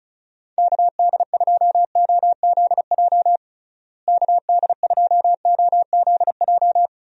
Morse code, heard as a keyed tone, about 34 wpm, 700 Hz. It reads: KD2OZJ KD2OZJ
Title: USA Call Signs - 6 Characters - Repeated 2x - 35wpm